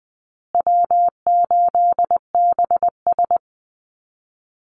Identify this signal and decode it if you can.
Morse code, heard as a keyed tone, about 20 wpm, 700 Hz. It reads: W8BS
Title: USA Call Signs Repeated 2x 20wpm